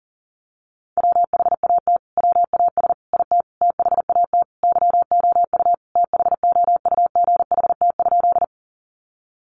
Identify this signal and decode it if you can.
Morse code, heard as a keyed tone, about 40 wpm, 700 Hz. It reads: WHAT WAS IT THAT YOU THOUGHT?
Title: Sentences from Top 300 Words 40wpm